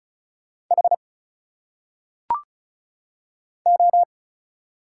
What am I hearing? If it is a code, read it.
Morse code, heard as a keyed tone, about 34 wpm, 700 Hz. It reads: H O
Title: Single Letters - Phonetic Pronunciation 35wpm